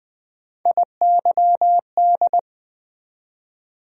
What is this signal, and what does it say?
Morse code, heard as a keyed tone, about 20 wpm, 700 Hz. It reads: IYD
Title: USA Call Sign Postfixes - 3 Letters 20wpm